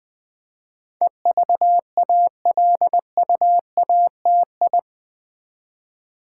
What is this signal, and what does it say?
Morse code, heard as a keyed tone, about 20 wpm, 700 Hz. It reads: EVALUATI
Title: Build A Word - 4 Letters At A Time - 10-Letter Words - 20wpm